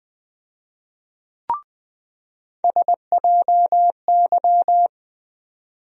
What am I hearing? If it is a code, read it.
Morse code, heard as a keyed tone, about 20 wpm, 700 Hz. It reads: SJY
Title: USA Call Sign Postfixes - 3 Letters 20wpm